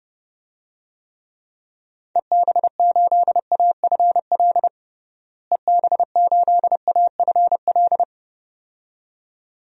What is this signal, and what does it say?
Morse code, heard as a keyed tone, about 32 wpm, 700 Hz. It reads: EB8AFL EB8AFL
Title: DXCC-100 Call Signs - 6 Characters - Repeated 2x - 30wpm